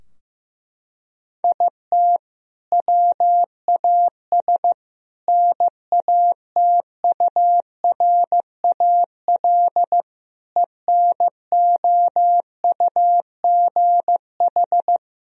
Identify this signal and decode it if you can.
Morse code, heard as a keyed tone, about 15 wpm, 700 Hz. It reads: IT WAS NATURAL ENOUGH
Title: Sentences from Top 1000 Words - Review Part 4 - 15wpm